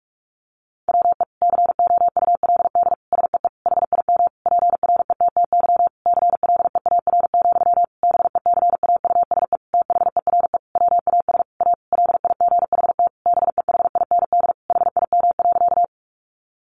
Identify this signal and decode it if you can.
Morse code, heard as a keyed tone, about 44 wpm, 700 Hz. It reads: WE COULD SEE HIM PRETTY CLEAR, BECAUSE THERE WAS A LIGHT BEHIND HIM.
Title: Sentences from Top 1000 Words 45wpm